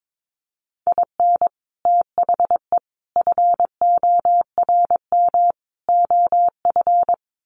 Morse code, heard as a keyed tone, about 22 wpm, 700 Hz. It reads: IN THE FORM OF
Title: Sets of 4 Words 22wpm